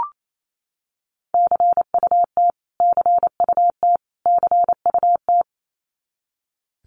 Morse code, heard as a keyed tone, about 28 wpm, 700 Hz. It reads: CUT CUT CUT
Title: Top 500 Words Repeated 28wpm